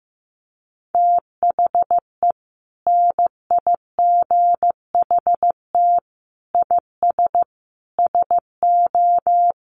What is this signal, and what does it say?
Morse code, heard as a keyed tone, about 15 wpm, 700 Hz. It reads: THE NIGHT IS SO
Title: Sentences from Top 500 Words 15wpm